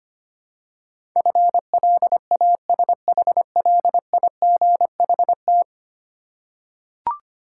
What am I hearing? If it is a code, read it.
Morse code, heard as a keyed tone, about 25 wpm, 700 Hz. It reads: FLASHLIGHT
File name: practice / 25wpm / Easy Words with Hints - 25wpm